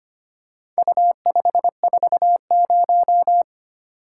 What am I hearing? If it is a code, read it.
Morse code, heard as a keyed tone, about 25 wpm, 700 Hz. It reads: U540
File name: practice / 25wpm / Four Letter-Number Combinations - 25wpm